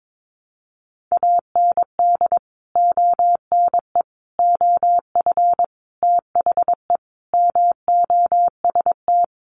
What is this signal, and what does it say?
Morse code, heard as a keyed tone, about 22 wpm, 700 Hz. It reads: AND ONE OF THE MOST
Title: Sets of 5 Words Encore 22wpm